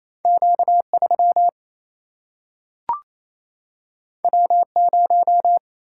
Morse code, heard as a keyed tone, about 28 wpm, 700 Hz. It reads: Q 3 W 0
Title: Single-Letter Single-Number 28wpm